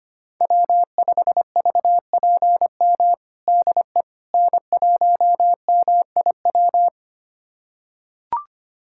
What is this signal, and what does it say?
Morse code, heard as a keyed tone, about 25 wpm, 700 Hz. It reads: W5VPM DE N1MIW
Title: USA Callsign DE Callsign - 5 Characters - 25wpm